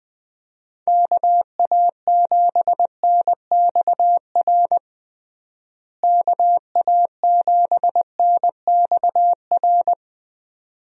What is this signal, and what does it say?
Morse code, heard as a keyed tone, about 21 wpm, 700 Hz. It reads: KA7NXR KA7NXR
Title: USA Call Signs - 6 Characters - Repeated 2x - 20wpm